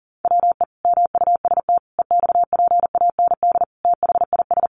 Morse code, heard as a keyed tone, about 40 wpm, 700 Hz. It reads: WE MUST EXPAND THIS
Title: Sentences from Top 1400 Words 40wpm